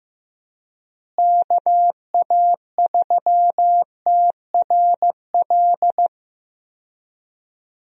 Morse code, heard as a keyed tone, about 15 wpm, 700 Hz. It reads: KA3TRL
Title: USA Call Signs - 6 Characters - Repeated 3x - 15wpm